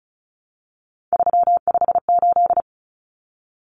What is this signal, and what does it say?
Morse code, heard as a keyed tone, about 34 wpm, 700 Hz. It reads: 358
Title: Numbers - 3-Digit 35wpm